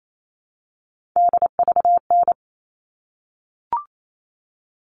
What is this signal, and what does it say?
Morse code, heard as a keyed tone, about 28 wpm, 700 Hz. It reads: DVN
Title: All Cut Numbers - 3-Digit 28wpm